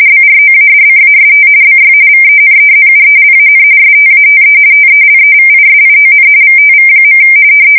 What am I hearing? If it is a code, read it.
RTTY, about 45.45 baud. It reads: BROADCAST BY VENEZUELAN STATE TELEVISION VTV.
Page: 170 Shift 2125/2295